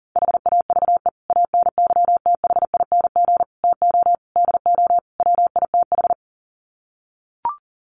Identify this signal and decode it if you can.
Morse code, heard as a keyed tone, about 40 wpm, 700 Hz. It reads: HAVE ANYTHING TO DO WITH
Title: Sets of 5 Words 40wpm